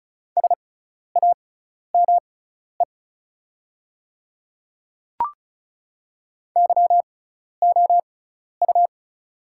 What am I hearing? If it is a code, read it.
Morse code, heard as a keyed tone, about 36 wpm, spaced as for 12 wpm, 700 Hz. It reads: SAME YOU
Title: 6x Character Spacing - Top 200 Words - 35wpm